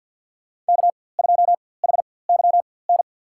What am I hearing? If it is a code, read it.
Morse code, heard as a keyed tone, about 53 wpm, spaced as for 24 wpm, 700 Hz. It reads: K2HYN
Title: USA Call Signs - 5 Characters - 4x Character Spacing - 50wpm